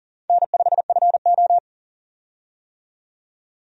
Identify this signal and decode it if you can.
Morse code, heard as a keyed tone, about 40 wpm, 700 Hz. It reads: N5FO
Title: USA Call Signs - 4 Characters - 40wpm